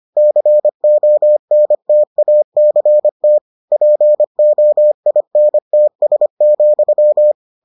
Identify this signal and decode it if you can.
Morse code, heard as a keyed tone, about 25 wpm, 600 Hz. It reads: CONTACT POINTS,